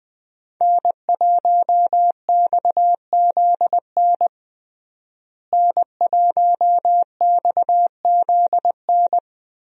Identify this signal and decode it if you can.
Morse code, heard as a keyed tone, about 20 wpm, 700 Hz. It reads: N1XZN N1XZN
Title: USA Call Signs - 5 Characters - Repeated 2x - 20wpm